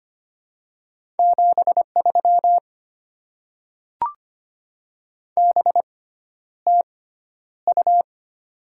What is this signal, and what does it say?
Morse code, heard as a keyed tone, about 25 wpm, spaced as for 11 wpm, 700 Hz. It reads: 73 BTU
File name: Top 20 QSO Elements - 6x Character Spacing - 25wpm